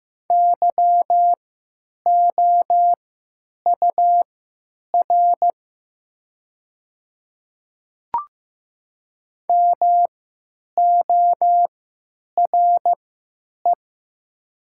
Morse code, heard as a keyed tone, about 15 wpm, spaced as for 9 wpm, 700 Hz. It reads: YOUR MORE
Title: 3x Character Spacing - Top 200 Words - 15wpm